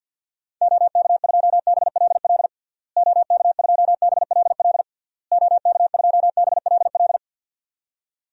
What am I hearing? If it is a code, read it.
Morse code, heard as a keyed tone, about 52 wpm, 700 Hz. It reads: OK2BLL OK2BLL OK2BLL
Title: DXCC-100 Call Signs - 6 Characters - Repeated 3x - 50wpm